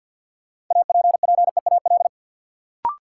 Morse code, heard as a keyed tone, about 53 wpm, 700 Hz. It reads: APPEAL
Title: Sentences from Top 1300 Words - New Word Drill Alt - 50wpm